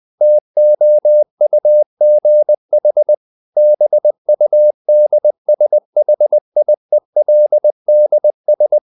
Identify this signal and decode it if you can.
Morse code, heard as a keyed tone, about 20 wpm, 600 Hz. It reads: TOUGH BUD SHIELDS
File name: the old man and the sea / the old man and the sea part 1 20wpm